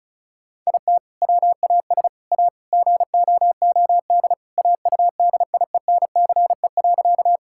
Morse code, heard as a keyed tone, about 36 wpm, 700 Hz. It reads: IT WAS A GOOD AUDIENCE.
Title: Sentences from Top 1000 Words - Review Part 5 - 35wpm